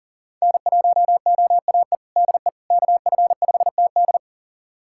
Morse code, heard as a keyed tone, about 40 wpm, 700 Hz. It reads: N1OAE DE KF5TD
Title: USA Callsign DE Callsign - 5 Characters - 40wpm